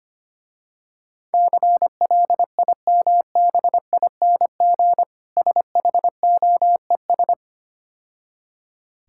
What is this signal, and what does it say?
Morse code, heard as a keyed tone, about 26 wpm, 700 Hz. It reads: CLIMBING SHOES
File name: practice / 25wpm / Difficult Words - Speed-Racing with Hints - 25wpm